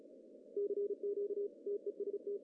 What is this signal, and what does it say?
Morse code, heard as a keyed tone, about 36 wpm, 400 Hz. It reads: CQ TEST
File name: CW / cqww-cw-k5tr